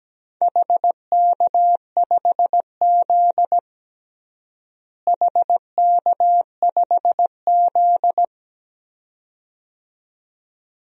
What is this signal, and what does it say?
Morse code, heard as a keyed tone, about 17 wpm, 700 Hz. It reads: HK5Z HK5Z
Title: DXCC-100 Call Signs - 4 Characters - Repeated 2x - 17wpm